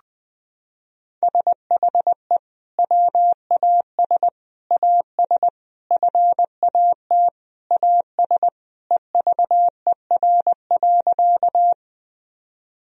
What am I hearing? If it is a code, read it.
Morse code, heard as a keyed tone, about 20 wpm, 700 Hz. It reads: SHE WAS AS FAT AS EVER.
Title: Sentences from Top 1000 Words - Review Part 4 - 20wpm